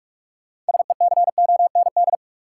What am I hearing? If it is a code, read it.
Morse code, heard as a keyed tone, about 46 wpm, 700 Hz. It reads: SECOND
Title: Sentences from Top 300 Words - New Word Drill - 45wpm